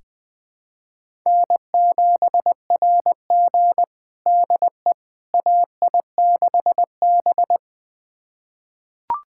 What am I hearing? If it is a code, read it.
Morse code, heard as a keyed tone, about 20 wpm, 700 Hz. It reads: N7RG DE AI6B
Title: USA Callsign DE Callsign - 4 Characters - 20wpm